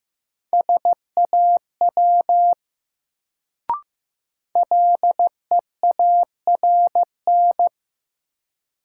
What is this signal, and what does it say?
Morse code, heard as a keyed tone, about 15 wpm, 700 Hz. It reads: SAW LEARN
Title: Sentences from Top 300 Words - New Word Drill - 15wpm